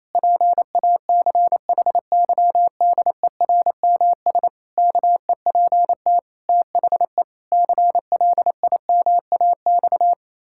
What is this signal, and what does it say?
Morse code, heard as a keyed tone, about 28 wpm, 700 Hz. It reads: PACHYDERMS KEPT THE CLIMAX
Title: Pangram Challenge v2 - 28wpm